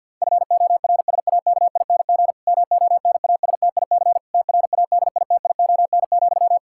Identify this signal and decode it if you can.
Morse code, heard as a keyed tone, about 52 wpm, 700 Hz. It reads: FORSAKING MONASTIC TRADITION,
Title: Pangram Challenge v2 - 50wpm